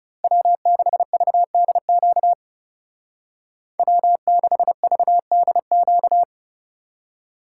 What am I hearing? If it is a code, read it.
Morse code, heard as a keyed tone, about 35 wpm, 700 Hz. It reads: W6VDQ W6VDQ
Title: Speed Racing - USA 1x3 Call Signs 30wpm